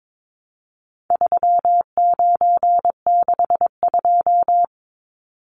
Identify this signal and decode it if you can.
Morse code, heard as a keyed tone, about 22 wpm, 700 Hz. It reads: 3962
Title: Numbers - 4-Digit 22wpm